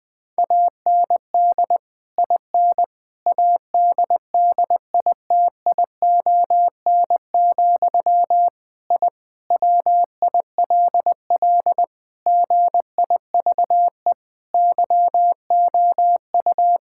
Morse code, heard as a keyed tone, about 20 wpm, 700 Hz. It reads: AND IN ADDITION, I WILL GIVE YOU
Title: Sentences from Top 1000 Words - Review Part 6 - 20wpm